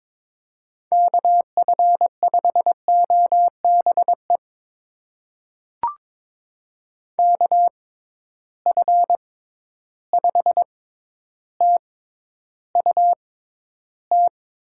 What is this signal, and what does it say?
Morse code, heard as a keyed tone, about 21 wpm, spaced as for 11 wpm, 700 Hz. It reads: KF5OBE KF5TUT
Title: USA Call Signs - 6 Characters - 6x Character Spacing - 22wpm